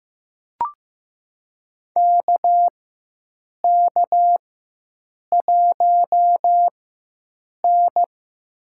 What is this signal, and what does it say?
Morse code, heard as a keyed tone, about 15 wpm, spaced as for 7 wpm, 700 Hz. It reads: KK1N
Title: USA Call Signs - 4 Characters - 4x Character Spacing - 15wpm